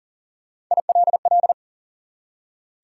Morse code, heard as a keyed tone, about 42 wpm, 700 Hz. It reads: ILL
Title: Three Letter Words - 40wpm